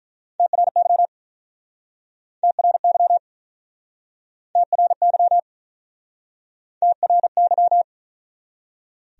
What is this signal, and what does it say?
Morse code, heard as a keyed tone, about 49 wpm, 700 Hz. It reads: TRY TRY TRY TRY